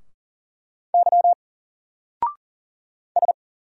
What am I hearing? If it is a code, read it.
Morse code, heard as a keyed tone, about 40 wpm, 700 Hz. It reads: Y S
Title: Single Letter-Number 40wpm